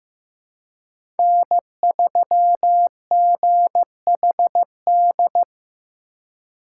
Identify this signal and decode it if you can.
Morse code, heard as a keyed tone, about 15 wpm, 700 Hz. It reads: N3GHD
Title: USA Call Signs - 5 Characters - 3x Character Spacing - 15wpm